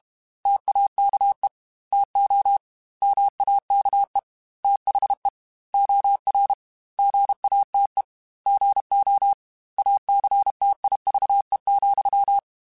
Morse code, heard as a keyed tone, about 32 wpm, 800 Hz. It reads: TAKE TO MAKE THE OR GATE GO ACTIVE,
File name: Technical Subjects / Technical Subjects-mp3s / OR Gate @30